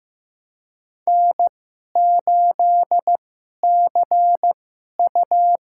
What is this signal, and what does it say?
Morse code, heard as a keyed tone, about 15 wpm, spaced as for 11 wpm, 700 Hz. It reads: N8CU
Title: USA Call Signs - 2x Character Spacing - 15wpm